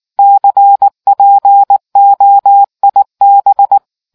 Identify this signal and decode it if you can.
Morse code, heard as a keyed tone, about 19 wpm, 800 Hz. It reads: CPOIB